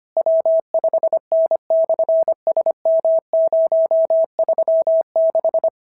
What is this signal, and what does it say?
Morse code, heard as a keyed tone, about 24 wpm, 650 Hz. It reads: W5N/SM036
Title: SOTA QSO Practice 25wpm